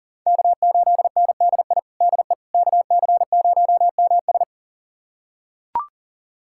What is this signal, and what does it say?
Morse code, heard as a keyed tone, about 40 wpm, 700 Hz. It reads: K8NDI DE KC0MS
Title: USA Callsign DE Callsign - 5 Characters - 40wpm